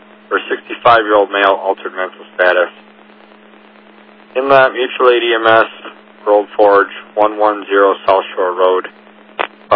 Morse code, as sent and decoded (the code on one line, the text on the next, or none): none